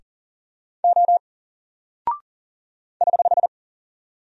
Morse code, heard as a keyed tone, about 40 wpm, 700 Hz. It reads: O <HH>
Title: Single Character (Letter:Number:Punctuation:Prosign) 40wpm